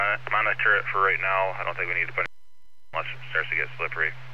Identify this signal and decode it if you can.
Emergency Alert System (SAME) tones, heard as harsh buzzing bursts, with speech. It is heard from 0.4 to 3.7 s.